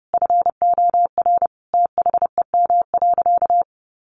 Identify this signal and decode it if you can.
Morse code, heard as a keyed tone, about 28 wpm, 700 Hz. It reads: FOR THEM.
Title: Sentences from Top 800 Words 30wpm